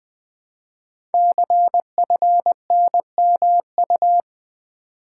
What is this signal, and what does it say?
Morse code, heard as a keyed tone, about 21 wpm, 700 Hz. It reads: CFNMU
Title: Code Groups - 5 Letters - 20wpm